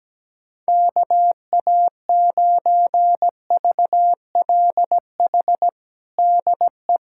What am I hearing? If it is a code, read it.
Morse code, heard as a keyed tone, about 17 wpm, 700 Hz. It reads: KA9VLH DE
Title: USA Callsign DE Callsign - 6 Characters - 17wpm